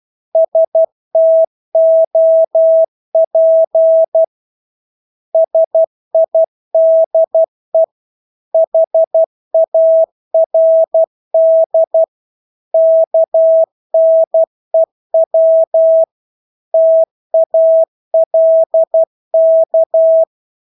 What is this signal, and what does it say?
Morse code, heard as a keyed tone, about 12 wpm, 650 Hz. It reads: STOP SIDE HARD KNEW TALK
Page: Engelske ord 12 wpm